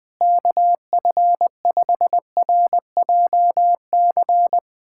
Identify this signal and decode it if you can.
Morse code, heard as a keyed tone, about 20 wpm, 700 Hz. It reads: KF5RJC
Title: USA Callsign DE Callsign 20wpm